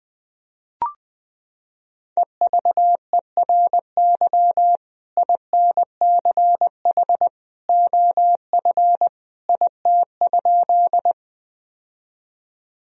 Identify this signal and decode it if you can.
Morse code, heard as a keyed tone, about 20 wpm, 700 Hz. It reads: EVERY INCH OF IT?
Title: Sentences from Top 1000 Words - Review Part 3 - 20wpm